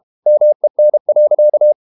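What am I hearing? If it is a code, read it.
Morse code, heard as a keyed tone, about 32 wpm, 600 Hz. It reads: MEN.
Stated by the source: Morse code practice at 32 wpm from CW med Gnister, reading Ken Follett's novel Never in English.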